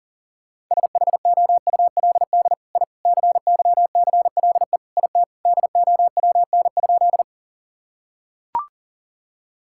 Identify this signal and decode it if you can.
Morse code, heard as a keyed tone, about 40 wpm, 700 Hz. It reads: SHOULD I CYCLE IT DOWN?
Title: Sentences from Top 1000 Words - Review Part 6 - 40wpm